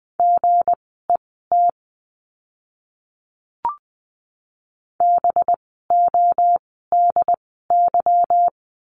Morse code, heard as a keyed tone, about 20 wpm, spaced as for 15 wpm, 700 Hz. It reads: GET BODY
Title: 185 Lesson 30f - Letter b - Words 20wpm